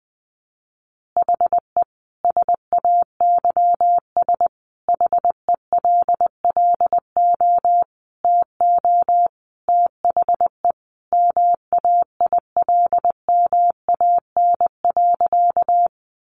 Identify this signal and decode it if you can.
Morse code, heard as a keyed tone, about 20 wpm, 700 Hz. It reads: HE SAYS HELLO TO THE MAILMAN.
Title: Additional Easy Phrases 20wpm